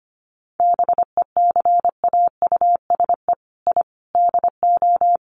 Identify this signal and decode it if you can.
Morse code, heard as a keyed tone, about 25 wpm, 700 Hz. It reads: BECAUSE I DO
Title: Sets of 4 Words 25wpm